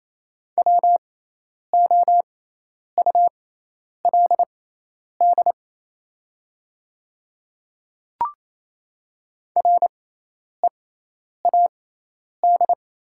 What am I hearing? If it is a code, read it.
Morse code, heard as a keyed tone, about 28 wpm, spaced as for 10 wpm, 700 Hz. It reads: WOULD READ
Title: 6x Character Spacing - Top 200 Words - 28wpm